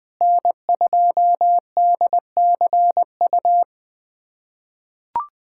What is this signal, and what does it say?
Morse code, heard as a keyed tone, about 21 wpm, 700 Hz. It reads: N2DCU
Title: USA Callsign DE Callsign - 5 Characters - 20wpm